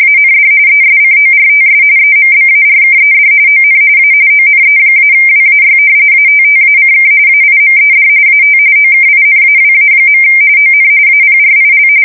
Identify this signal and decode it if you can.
RTTY, about 45.45 baud. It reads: AND ESCORTS EMBARKED ON A ONE-WAY ATTACK AIMED AT OKINAWA. ON APRIL
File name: fsk170 / battleshipnj